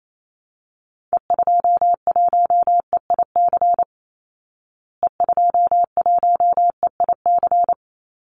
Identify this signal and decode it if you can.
Morse code, heard as a keyed tone, about 28 wpm, 700 Hz. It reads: E21EIC E21EIC
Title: Call Signs Repeated 3x - 28wpm